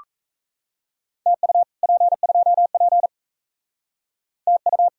Morse code, heard as a keyed tone, about 35 wpm, 700 Hz. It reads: TU P2P TU
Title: Speed-Racing Common QSO Phrases - 28wpm